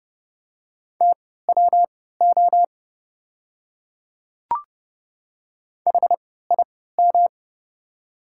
Morse code, heard as a keyed tone, about 30 wpm, spaced as for 17 wpm, 700 Hz. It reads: TWO HIM
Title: 3x Character Spacing - Top 100 Words - 30wpm